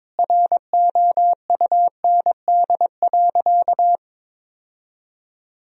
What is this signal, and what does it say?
Morse code, heard as a keyed tone, about 22 wpm, 700 Hz. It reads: ROUND.
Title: Sentences from Top 1000 Words - Review Part 1 - 22wpm